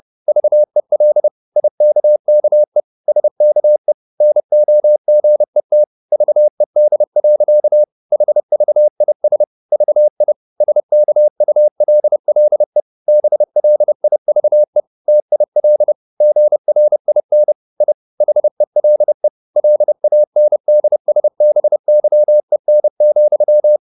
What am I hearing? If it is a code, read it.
Morse code, heard as a keyed tone, about 30 wpm, 600 Hz. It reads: VEL IKKE SKE NOGET VED. HVIS VI SKULLE BLIVE TIL GRIN I HELE LANDSBYEN,
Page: Ferie 30 wpm